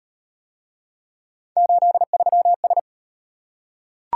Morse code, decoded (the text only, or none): 83S